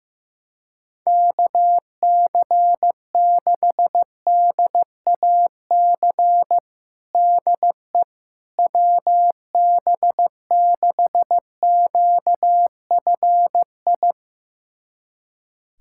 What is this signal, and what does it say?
Morse code, heard as a keyed tone, about 15 wpm, 700 Hz. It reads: KC6DAC DE WB6QFI
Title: USA Callsign DE Callsign - 6 Characters - 15wpm